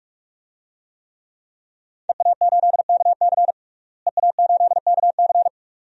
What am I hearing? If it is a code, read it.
Morse code, heard as a keyed tone, about 46 wpm, 700 Hz. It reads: EA8KC EA8KC
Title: DXCC-100 Call Signs - 5 Characters - Repeated 2x - 45wpm